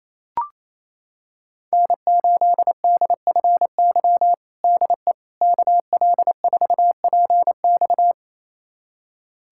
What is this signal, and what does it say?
Morse code, heard as a keyed tone, about 28 wpm, 700 Hz. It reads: N8DFY DE KL4PX
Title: USA Callsign DE Callsign - 5 Characters - 28wpm